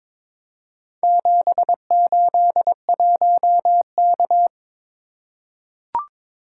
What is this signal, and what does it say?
Morse code, heard as a keyed tone, about 22 wpm, 700 Hz. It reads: 781K
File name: practice / 22wpm / Four Letter-Number Combinations - 22wpm